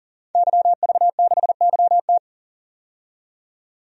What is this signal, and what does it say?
Morse code, heard as a keyed tone, about 40 wpm, 700 Hz. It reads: YV6YT
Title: DXCC-100 Call Signs - 5 Characters - 40wpm